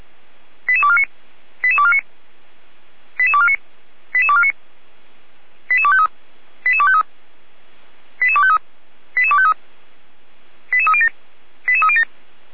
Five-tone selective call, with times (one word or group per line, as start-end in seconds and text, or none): 0.7-1.0 88249
1.6-2.0 88249
3.2-3.6 88249
4.1-4.5 88249
5.7-6.1 88253
6.7-7.0 88253
8.2-8.6 88253
9.2-9.5 88253
10.7-11.1 88307
11.7-12.0 88307